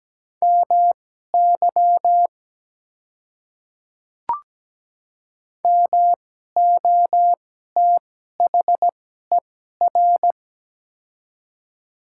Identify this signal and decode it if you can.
Morse code, heard as a keyed tone, about 17 wpm, spaced as for 12 wpm, 700 Hz. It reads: MY MOTHER
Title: 2x Character Spacing - Top 200 Words - 17wpm